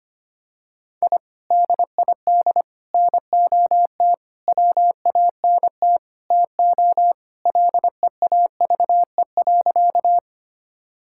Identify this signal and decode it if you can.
Morse code, heard as a keyed tone, about 25 wpm, 700 Hz. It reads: I DID NOT WANT TO LEAVE.
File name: practice / 25wpm / Sentences with 4-Word Phrases - 25wpm